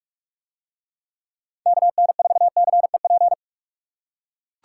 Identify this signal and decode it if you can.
Morse code, heard as a keyed tone, about 42 wpm, 700 Hz. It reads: KN4CEP